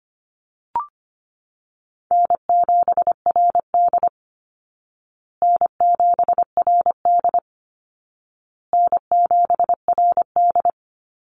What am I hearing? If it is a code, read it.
Morse code, heard as a keyed tone, about 25 wpm, 700 Hz. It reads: N7RD N7RD N7RD
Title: Easy Call Signs Repeated 3x - 25wpm